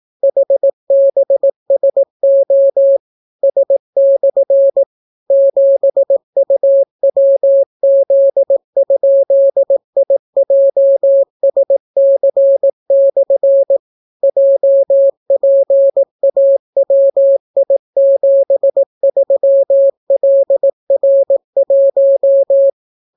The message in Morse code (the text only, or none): HBSO S/ 7UWZ?IJSC/ JPAWI73LR1